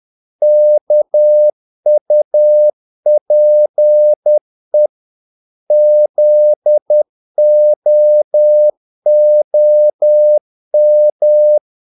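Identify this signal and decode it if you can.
Morse code, heard as a keyed tone, about 10 wpm, 600 Hz. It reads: KUPE ZOOM
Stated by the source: Morse practice of short Danish words at 10 wpm, from CW med Gnister.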